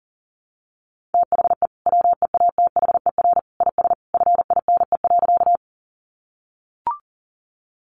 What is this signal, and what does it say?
Morse code, heard as a keyed tone, about 40 wpm, 700 Hz. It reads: THE WEATHER IS FINE.
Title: Sentences from Top 800 Words 40wpm